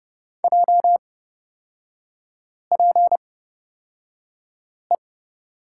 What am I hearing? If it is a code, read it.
Morse code, heard as a keyed tone, about 30 wpm, spaced as for 5 wpm, 700 Hz. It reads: JPE
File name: Lesson 36e - BT - Three Characters 30wpm